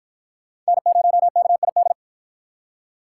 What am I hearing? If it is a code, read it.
Morse code, heard as a keyed tone, about 45 wpm, 700 Hz. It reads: N0KID